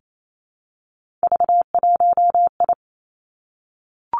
Morse code, heard as a keyed tone, about 28 wpm, 700 Hz. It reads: V1I
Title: Three Challenging Letter-Number Combinations 28wpm